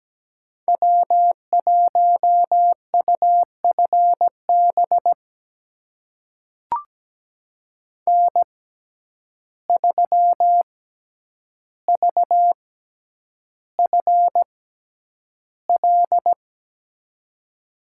Morse code, heard as a keyed tone, about 17 wpm, spaced as for 9 wpm, 700 Hz. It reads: W1UFB N3VFL
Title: USA Call Signs - 5 Characters - 6x Character Spacing - 17wpm